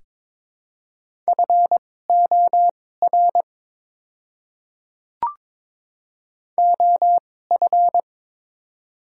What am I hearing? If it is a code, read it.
Morse code, heard as a keyed tone, about 23 wpm, spaced as for 16 wpm, 700 Hz. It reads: FOR OF
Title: 2x Character Spacing - Top 300 Words - 22wpm